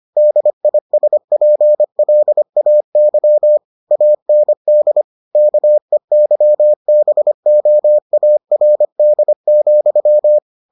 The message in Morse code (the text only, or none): DISPLAY AND KEYBOARD,